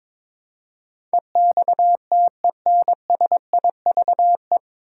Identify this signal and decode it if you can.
Morse code, heard as a keyed tone, about 22 wpm, 700 Hz. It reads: EXTENSIVE
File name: 9 Letter Words - First 3 Letters Given - 22wpm